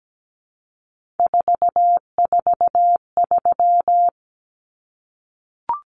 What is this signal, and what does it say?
Morse code, heard as a keyed tone, about 17 wpm, 700 Hz. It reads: 443
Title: Common Cut Numbers - 3-Digit 17wpm